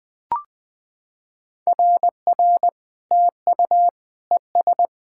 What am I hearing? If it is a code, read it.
Morse code, heard as a keyed tone, about 20 wpm, 700 Hz. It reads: RR TU ES
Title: SOTA QSO Practice 20wpm